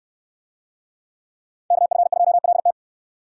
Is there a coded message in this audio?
Yes: Morse code from 1.7 to 2.7 s, heard as a keyed tone, about 68 wpm, 700 Hz. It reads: KF2LT